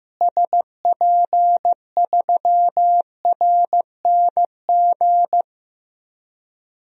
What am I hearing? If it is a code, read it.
Morse code, heard as a keyed tone, about 15 wpm, 700 Hz. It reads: SP3RNG
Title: DXCC-100 Call Signs - 6 Characters - Repeated 3x - 15wpm